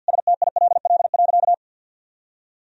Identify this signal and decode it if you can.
Morse code, heard as a keyed tone, about 49 wpm, 700 Hz. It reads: STILL.
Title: Sentences from Top 400 Words and Abbreviations 50wpm